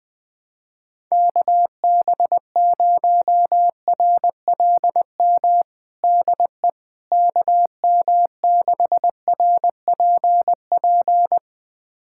Morse code, heard as a keyed tone, about 21 wpm, 700 Hz. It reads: KB0RLM DE KM6RPP
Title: USA Callsign DE Callsign - 6 Characters - 20wpm